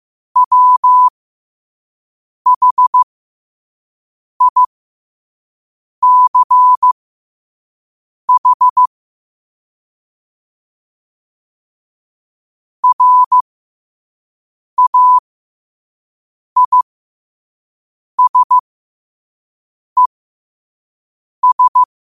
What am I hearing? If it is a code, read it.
Morse code, heard as a keyed tone, about 15 wpm, spaced as for 5 wpm, 1000 Hz. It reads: WHICH RAISES
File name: qotdfiles / 5wpm